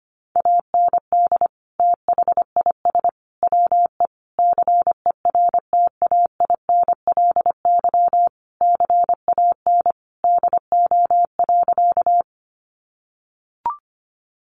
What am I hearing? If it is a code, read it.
Morse code, heard as a keyed tone, about 25 wpm, 700 Hz. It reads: AND THIS WE CERTAINLY CAN DO.
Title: Sentences from Top 900 Words 25wpm